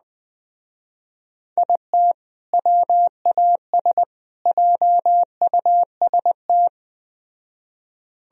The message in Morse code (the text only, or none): IT WAS JUST